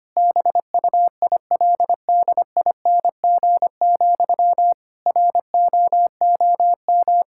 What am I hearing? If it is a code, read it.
Morse code, heard as a keyed tone, about 25 wpm, 700 Hz. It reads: BUILDING, ROOM
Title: Sentences from Top 1000 Words - Review Part 5 - 25wpm